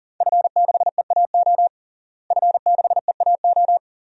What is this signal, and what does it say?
Morse code, heard as a keyed tone, about 40 wpm, 700 Hz. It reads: F6EAO F6EAO
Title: DXCC-100 Call Signs - 5 Characters - Repeated 2x - 40wpm